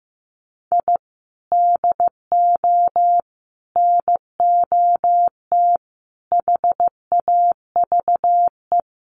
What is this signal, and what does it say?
Morse code, heard as a keyed tone, about 15 wpm, 700 Hz. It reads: I DO NOT HAVE
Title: Sets of 4 Words 15wpm